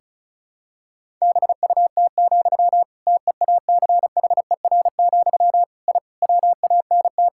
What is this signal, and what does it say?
Morse code, heard as a keyed tone, about 36 wpm, 700 Hz. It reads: BUT, TEACHER, I WANT
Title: Sentences from Top 1000 Words - Review Part 6 - 35wpm